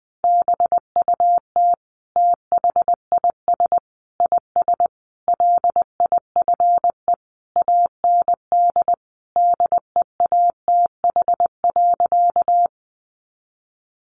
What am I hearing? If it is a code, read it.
Morse code, heard as a keyed tone, about 20 wpm, 700 Hz. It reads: BUT THIS IS LIFE AND DEATH.
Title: Sentences from Top 1100 Words 20wpm